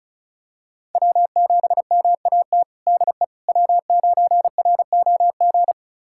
Morse code, heard as a keyed tone, about 34 wpm, 700 Hz. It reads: W7MAT DE W9ROG
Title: USA Callsign DE Callsign - 5 Characters - 35wpm